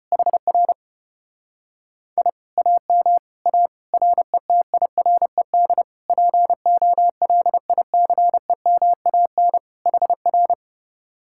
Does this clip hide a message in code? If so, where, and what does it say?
Yes: Morse code from 0.1 to 10.5 s, heard as a keyed tone, about 33 wpm, 700 Hz. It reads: HR I AM A RETIRED POLICEMAN HR